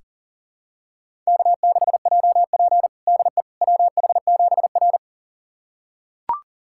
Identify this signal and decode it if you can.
Morse code, heard as a keyed tone, about 40 wpm, 700 Hz. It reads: K6JP DE WH7R
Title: USA Callsign DE Callsign - 4 Characters - 40wpm